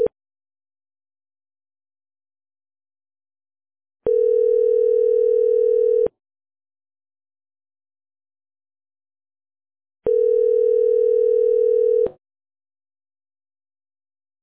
Ringback tone (North America), 440 + 480 Hz.